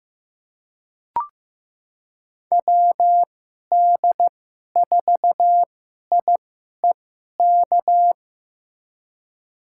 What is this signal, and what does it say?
Morse code, heard as a keyed tone, about 15 wpm, spaced as for 11 wpm, 700 Hz. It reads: WD4IEK
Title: USA Call Signs - 6 Characters - 2x Character Spacing - 15wpm